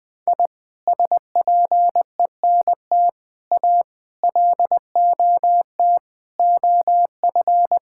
Morse code, heard as a keyed tone, about 21 wpm, 700 Hz. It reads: I SPENT A LOT OF
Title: Sentences with 6-Word Phrases - 20wpm